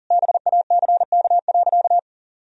Morse code, heard as a keyed tone, about 40 wpm, 700 Hz. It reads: BACK.
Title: Sentences from Top 1000 Words - Review Part 2 - 40wpm